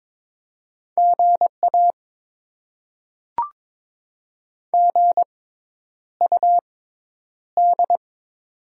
Morse code, heard as a keyed tone, about 22 wpm, spaced as for 10 wpm, 700 Hz. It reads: GA GUD
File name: Top 20 QSO Elements - 6x Character Spacing - 22wpm